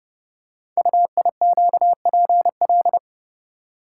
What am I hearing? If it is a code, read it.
Morse code, heard as a keyed tone, about 29 wpm, 700 Hz. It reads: UIQPL
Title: Code Groups - 5 Letters - 30wpm